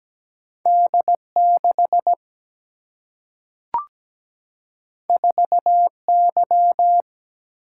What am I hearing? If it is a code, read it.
Morse code, heard as a keyed tone, about 17 wpm, 700 Hz. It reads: D 6 4 Y
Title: Single-Letter Single-Number 17wpm